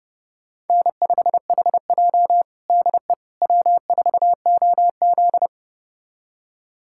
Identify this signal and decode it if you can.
Morse code, heard as a keyed tone, about 30 wpm, 700 Hz. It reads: N5HJ DE W4OZ
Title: USA Callsign DE Callsign - 4 Characters - 30wpm